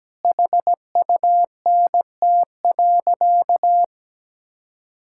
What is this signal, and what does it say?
Morse code, heard as a keyed tone, about 17 wpm, 700 Hz. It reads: HUNT.
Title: Sentences from Top 1600 Words 17wpm